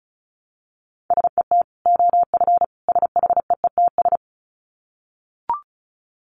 Morse code, heard as a keyed tone, about 34 wpm, 700 Hz. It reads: SET OF SHEETS
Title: Dit Challenge 35wpm